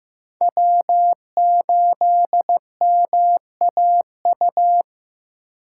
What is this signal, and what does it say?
Morse code, heard as a keyed tone, about 16 wpm, 700 Hz. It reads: W8MAU
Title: USA Call Signs - 5 Characters - 3x Character Spacing - 15wpm